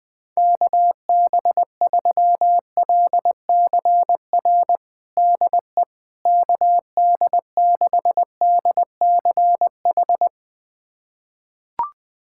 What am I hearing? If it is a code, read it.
Morse code, heard as a keyed tone, about 20 wpm, 700 Hz. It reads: KB3LCR DE KD6DCH
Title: USA Callsign DE Callsign - 6 Characters - 20wpm